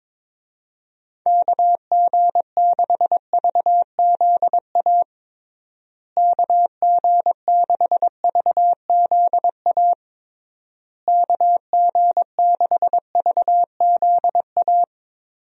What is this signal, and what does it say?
Morse code, heard as a keyed tone, about 22 wpm, 700 Hz. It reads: KG6VZA KG6VZA KG6VZA
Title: USA Call Signs - 6 Characters - Repeated 3x - 22wpm